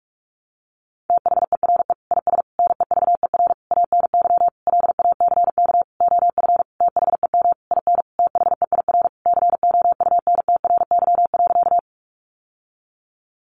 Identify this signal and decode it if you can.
Morse code, heard as a keyed tone, about 44 wpm, 700 Hz. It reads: THERE IS NEVER ANY LACK OF THEM IN THEIR COUNTRY.
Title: Sentences from Top 1100 Words 45wpm